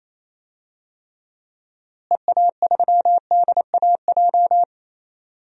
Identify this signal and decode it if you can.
Morse code, heard as a keyed tone, about 28 wpm, 700 Hz. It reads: EA3DAJ
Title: DXCC-100 Call Signs - 6 Characters - Repeated 3x - 28wpm